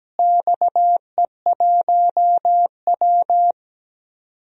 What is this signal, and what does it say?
Morse code, heard as a keyed tone, about 17 wpm, 700 Hz. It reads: XE1W
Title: DXCC-100 Call Signs - 4 Characters - Repeated 2x - 17wpm